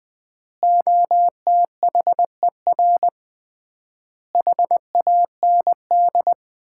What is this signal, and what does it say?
Morse code, heard as a keyed tone, about 20 wpm, 700 Hz. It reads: OTHER HAND
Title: 3x Word-Spacing - Sets of 5 Words - 20wpm